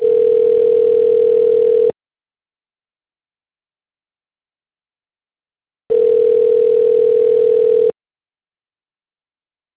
Ringback tone (North America), 440 + 480 Hz.